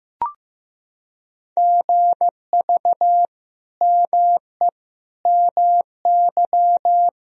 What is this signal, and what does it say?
Morse code, heard as a keyed tone, about 15 wpm, 700 Hz. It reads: GV ME MY
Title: Sentences from Top 200 Words and Abbreviations 15wpm